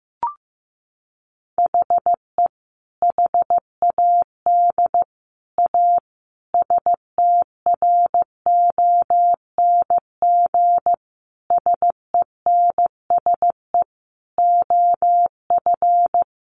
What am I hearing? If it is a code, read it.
Morse code, heard as a keyed tone, about 15 wpm, 700 Hz. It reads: HE HAD A STRONG SENSE OF
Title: Sentences from Top 1500 Words 15wpm